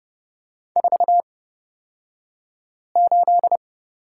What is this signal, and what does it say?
Morse code, heard as a keyed tone, about 30 wpm, spaced as for 5 wpm, 700 Hz. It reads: 48
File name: Lesson 37d - Letter x - Two Characters 30wpm